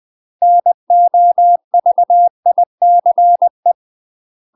Morse code, heard as a keyed tone, about 20 wpm, 700 Hz. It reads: NOVICE
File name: UK MASTER QSOs TRAINING / 20WPM / UK MASTER QSOs 0069